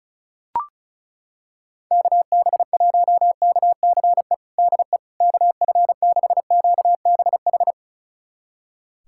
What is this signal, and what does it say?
Morse code, heard as a keyed tone, about 36 wpm, 700 Hz. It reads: KB1KCE DE KF6QBH
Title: USA Callsign DE Callsign 35wpm